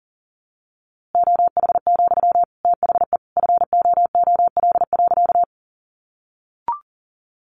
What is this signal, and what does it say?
Morse code, heard as a keyed tone, about 40 wpm, 700 Hz. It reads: OH, THE FOOL.
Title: A Sound Of Thunder 40wpm